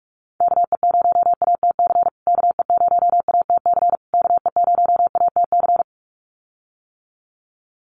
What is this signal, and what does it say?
Morse code, heard as a keyed tone, about 44 wpm, 690 Hz. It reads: KE0ATC KE0ATC KE0ATC
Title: USA Call Signs Repeated 3x 45wpm